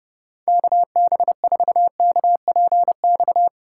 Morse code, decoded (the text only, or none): KB4KPX